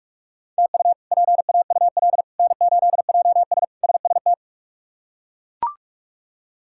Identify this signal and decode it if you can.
Morse code, heard as a keyed tone, about 41 wpm, 700 Hz. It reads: TU PAUL N8JS SST
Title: K1USN SST QSO Practice - 45wpm